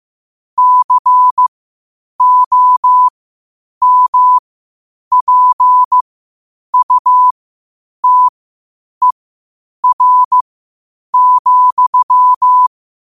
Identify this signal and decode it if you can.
Morse code, heard as a keyed tone, about 15 wpm, spaced as for 8 wpm, 1000 Hz. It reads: COMPUTER,
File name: qotdfiles / 8wpm